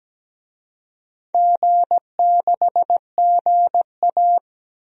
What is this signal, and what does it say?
Morse code, heard as a keyed tone, about 17 wpm, 700 Hz. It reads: G6GA
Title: DXCC-100 Call Signs - 4 Characters - 17wpm